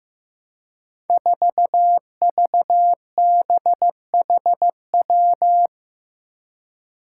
Morse code, heard as a keyed tone, about 15 wpm, 700 Hz. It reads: 4VBHW
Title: Five Challenging Letter-Number Combinations 15wpm